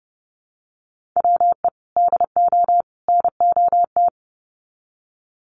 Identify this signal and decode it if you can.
Morse code, heard as a keyed tone, about 30 wpm, 700 Hz. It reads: WE DO NOT
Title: Sets of 3 Words 30wpm